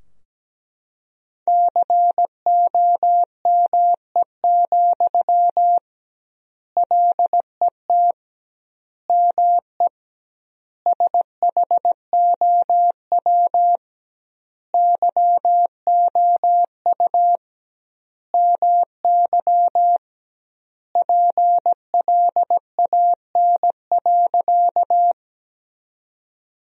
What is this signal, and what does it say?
Morse code, heard as a keyed tone, about 17 wpm, 700 Hz. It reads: COME, LET ME SHOW YOU MY PLAN.
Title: 2x Word-Spacing - Sentences from Top 500 Words - 17wpm